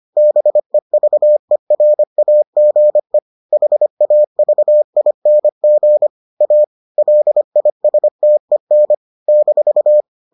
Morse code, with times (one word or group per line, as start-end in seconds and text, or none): 0.2-3.2 BEVERAGE
3.5-6.1 HAVING
6.4-6.6 A
7.0-8.9 LISTEN
9.3-10.0 -